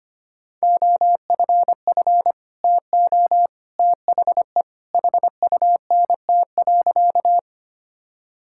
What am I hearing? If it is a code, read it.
Morse code, heard as a keyed tone, about 25 wpm, 700 Hz. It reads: OFF TO THE HUNT.
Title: Sentences from Top 1600 Words 25wpm